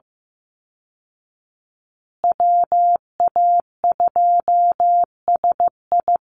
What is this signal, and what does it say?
Morse code, heard as a keyed tone, about 15 wpm, 700 Hz. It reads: WA2SI
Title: Call Signs Repeated 3x - 15wpm